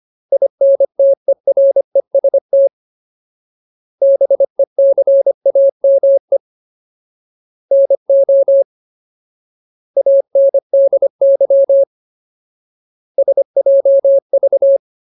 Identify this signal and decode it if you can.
Morse code, heard as a keyed tone, about 25 wpm, 550 Hz. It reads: INTEREST BECAME NO ANDY SJV
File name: Ultimate Word List / 25WPM Ultimate